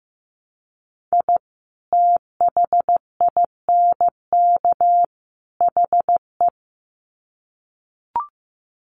Morse code, heard as a keyed tone, about 15 wpm, 700 Hz. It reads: I THINK HE
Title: Sets of 3 Words 15wpm